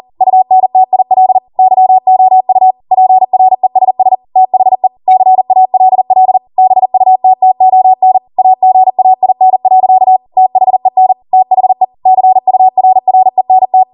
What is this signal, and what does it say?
Morse code, heard as a keyed tone, about 40 wpm, 750 Hz. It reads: UNTIL YOU PRESS THE CALL BUTTON AGAIN. THEN THE CURRENT